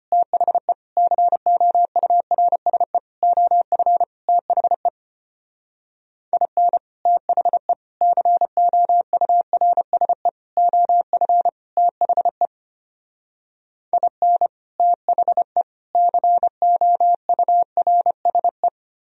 Morse code, heard as a keyed tone, about 33 wpm, 700 Hz. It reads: THE COURSE OF THE IN THE COURSE OF THE IN THE COURSE